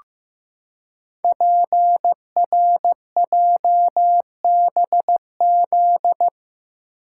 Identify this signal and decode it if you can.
Morse code, heard as a keyed tone, about 15 wpm, 700 Hz. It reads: PRJBZ
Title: Code Groups - 5 Letters - 15wpm